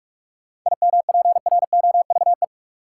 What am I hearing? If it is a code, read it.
Morse code, heard as a keyed tone, about 43 wpm, 700 Hz. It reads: IMPROVE
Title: Sentences from Top 800 Words - New Word Drill - 45wpm